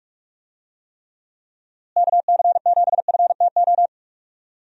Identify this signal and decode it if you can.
Morse code, heard as a keyed tone, about 44 wpm, 700 Hz. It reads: KC7FTO
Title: USA Call Signs - 6 Characters - Repeated 2x - 45wpm